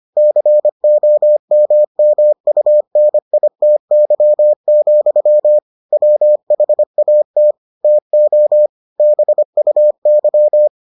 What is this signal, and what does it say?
Morse code, heard as a keyed tone, about 25 wpm, 600 Hz. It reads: COMMUNITY, WHAT TO BUY